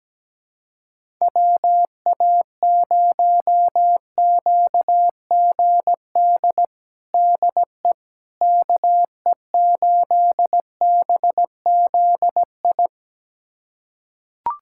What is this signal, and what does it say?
Morse code, heard as a keyed tone, about 17 wpm, 700 Hz. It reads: WA0QGD DE KE8BZI
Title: USA Callsign DE Callsign 17wpm